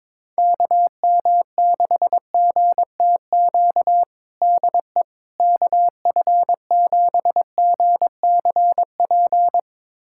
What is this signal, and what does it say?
Morse code, heard as a keyed tone, about 21 wpm, 700 Hz. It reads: KM6GTQ DE KF7GCP
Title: USA Callsign DE Callsign - 6 Characters - 22wpm